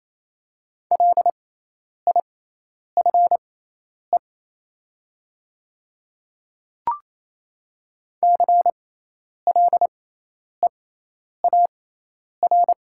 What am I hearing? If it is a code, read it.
Morse code, heard as a keyed tone, about 28 wpm, spaced as for 10 wpm, 700 Hz. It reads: LIFE CLEAR
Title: 6x Character Spacing - Top 500 Words - 28wpm